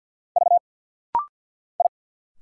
Morse code, heard as a keyed tone, about 49 wpm, 700 Hz. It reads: V I
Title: Single Letters - Rapid-Fire 50wpm